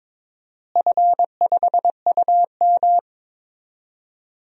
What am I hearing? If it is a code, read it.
Morse code, heard as a keyed tone, about 22 wpm, 700 Hz. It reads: F5UM
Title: DXCC-100 Call Signs - 4 Characters - Repeated 2x - 22wpm